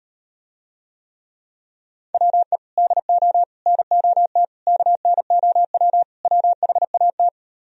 Morse code, heard as a keyed tone, about 38 wpm, 700 Hz. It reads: WE DO NOT KNOW WHAT